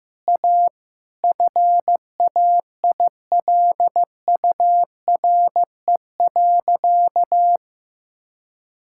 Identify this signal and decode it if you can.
Morse code, heard as a keyed tone, about 15 wpm, 700 Hz. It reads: A FAILURE.
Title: Sentences from Top 1000 Words - Review Part 6 - 15wpm